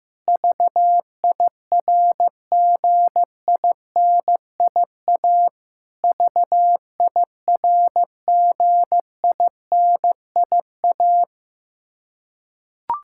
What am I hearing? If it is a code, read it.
Morse code, heard as a keyed tone, about 15 wpm, 700 Hz. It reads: VIRGINIA VIRGINIA
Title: Practice QSO - Exchange 1 - Beginning - 15wpm